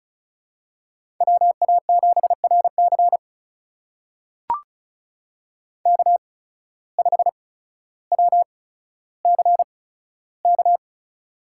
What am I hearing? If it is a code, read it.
Morse code, heard as a keyed tone, about 36 wpm, spaced as for 15 wpm, 700 Hz. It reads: WA7RC K5WCK
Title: USA Call Signs - 5 Characters - 8x Character Spacing - 35wpm